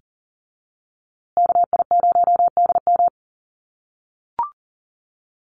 Morse code, heard as a keyed tone, about 40 wpm, 700 Hz. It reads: KI0DM
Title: USA Call Signs Repeated 3x 40wpm